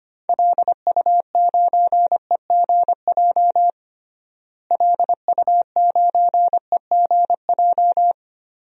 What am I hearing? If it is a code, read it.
Morse code, heard as a keyed tone, about 25 wpm, 700 Hz. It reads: LU9EGJ LU9EGJ
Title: DXCC-100 Call Signs - 6 Characters - Repeated 2x - 25wpm